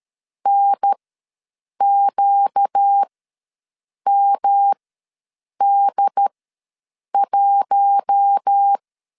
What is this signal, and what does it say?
Morse code, heard as a keyed tone, about 13 wpm, spaced as for 7 wpm, 800 Hz. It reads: NQMD1
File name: LEZIONI / 5 LBVC / 5 LB ws 12wpm 3034